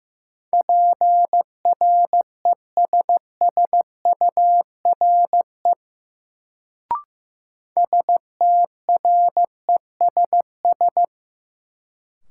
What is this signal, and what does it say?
Morse code, heard as a keyed tone, about 15 wpm, 700 Hz. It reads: PRESSURE STRESS
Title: Sentences from Top 1000 Words - Review Part 6 - New Word Drill - 15wpm